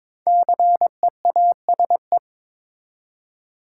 Morse code, heard as a keyed tone, about 22 wpm, 700 Hz. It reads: CEASE
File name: 5 Letter Words - First 3 Letters Given - 22wpm